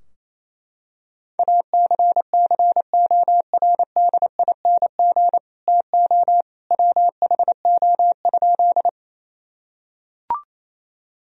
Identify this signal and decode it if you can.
Morse code, heard as a keyed tone, about 27 wpm, 700 Hz. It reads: ACCORDING TO WHO?
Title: Sentences from Top 1000 Words - Review Part 4 - 28wpm